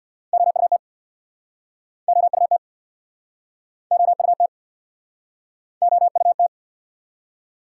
Morse code, heard as a keyed tone, about 69 wpm, 700 Hz. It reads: OUT OUT OUT OUT